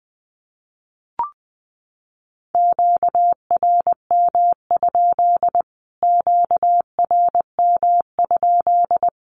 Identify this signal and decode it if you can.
Morse code, heard as a keyed tone, about 20 wpm, 700 Hz. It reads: QRM? QRM?
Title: Q-Codes Repeated 3x 20wpm